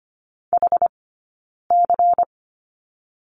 Morse code, heard as a keyed tone, about 24 wpm, 700 Hz. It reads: H C
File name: course / 25:10wpm / 092 Lesson 16d - Letter c - Two Characters 25wpm